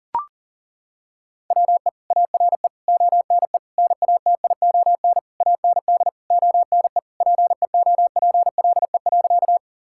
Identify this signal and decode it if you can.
Morse code, heard as a keyed tone, about 40 wpm, 700 Hz. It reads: WE ARE ONE NATION AND ONE PEOPLE.
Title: Sentences from Top 1000 Words - Review Part 6 - 40wpm